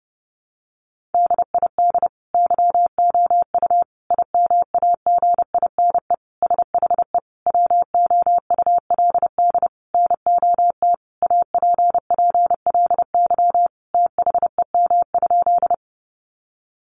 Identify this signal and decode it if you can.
Morse code, heard as a keyed tone, about 30 wpm, 700 Hz. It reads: DID YOU IMAGINE SHE WOULD NOT APPLY THEM?
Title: Sentences from Top 1400 Words 30wpm